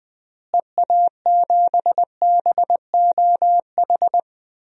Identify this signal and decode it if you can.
Morse code, heard as a keyed tone, about 20 wpm, 700 Hz. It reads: EA7BOH
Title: DXCC-100 Call Signs - 6 Characters - Repeated 3x - 20wpm